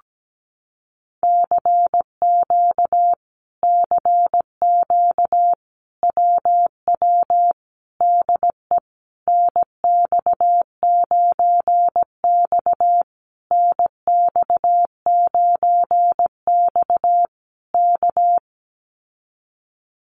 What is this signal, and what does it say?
Morse code, heard as a keyed tone, about 17 wpm, 700 Hz. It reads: CQ CQ WW DE NX9X NX9X K
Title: Common QSO Phrases Encore 17wpm